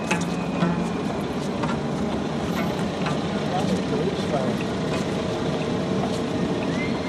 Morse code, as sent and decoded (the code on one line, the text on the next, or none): none